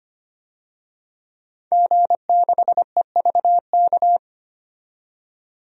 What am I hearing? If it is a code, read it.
Morse code, heard as a keyed tone, about 25 wpm, 700 Hz. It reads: G6EVK
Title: DXCC-100 Call Signs - 5 Characters - 25wpm